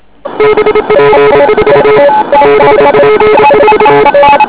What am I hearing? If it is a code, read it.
Morse code, heard as a keyed tone, about 27 wpm, 430 Hz, with noise. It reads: BP4 BP4
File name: OK QRP 2007 / REC27